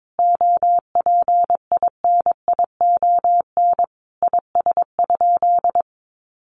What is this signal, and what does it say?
Morse code, heard as a keyed tone, about 22 wpm, 700 Hz. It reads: OPINION IS?
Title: Sentences from Top 1200 Words 22wpm